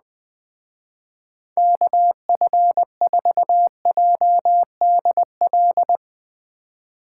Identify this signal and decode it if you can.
Morse code, heard as a keyed tone, about 20 wpm, 700 Hz. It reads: KF4JDL
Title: Speed Racing - USA 2x3 Call Signs 17wpm